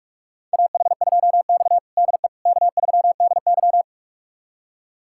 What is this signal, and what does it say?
Morse code, heard as a keyed tone, about 46 wpm, 700 Hz. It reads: AH2X DE K3DY
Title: USA Callsign DE Callsign - 4 Characters - 45wpm